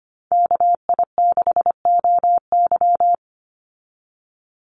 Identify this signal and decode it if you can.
Morse code, heard as a keyed tone, about 25 wpm, 700 Hz. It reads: KI6OY
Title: Call Signs - 25wpm